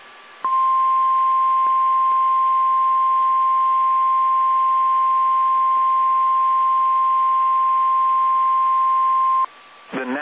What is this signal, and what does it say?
Emergency Alert System (SAME) tones (the 1050 Hz attention tone). It is heard from 0.4 to 9.5 s.
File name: eas / KSDZKDJL / KSDZKDJL